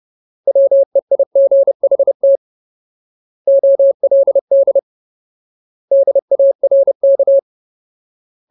Morse code, heard as a keyed tone, about 30 wpm, 550 Hz. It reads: WEIGHT OLD DARK
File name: Ultimate Word List / 30WPM Ultimate